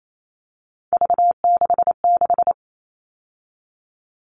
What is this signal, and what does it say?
Morse code, heard as a keyed tone, about 28 wpm, 700 Hz. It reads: V66
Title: All Cut Numbers - 3-Digit 28wpm